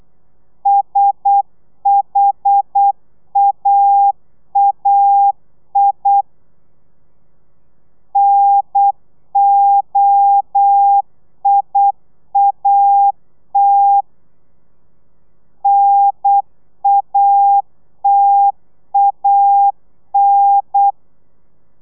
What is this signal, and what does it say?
Morse code, heard as a keyed tone, about 8 wpm, 800 Hz. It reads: SHAAI NOIAT NATAN